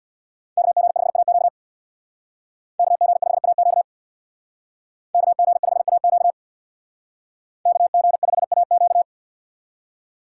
Morse code, heard as a keyed tone, about 63 wpm, 700 Hz. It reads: KG5AQ KG5AQ KG5AQ KG5AQ